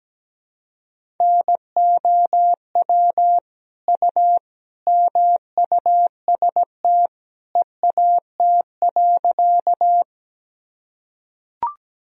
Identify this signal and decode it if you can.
Morse code, heard as a keyed tone, about 17 wpm, 700 Hz. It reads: NOW U MUST EAT.
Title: Sentences from Top 300 Words and Abbreviations 17wpm